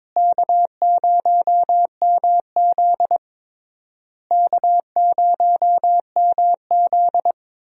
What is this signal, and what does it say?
Morse code, heard as a keyed tone, about 21 wpm, 700 Hz. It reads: K0MZ K0MZ
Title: USA Call Signs - 4 Characters - Repeated 3x - 22wpm